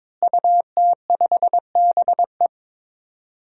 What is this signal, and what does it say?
Morse code, heard as a keyed tone, about 22 wpm, 700 Hz. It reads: UT5BE
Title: DXCC-100 Call Signs - 5 Characters - Repeated 2x - 22wpm